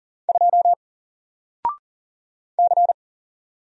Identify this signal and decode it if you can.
Morse code, heard as a keyed tone, about 40 wpm, 700 Hz. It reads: 2 C
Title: Single Letter-Number 40wpm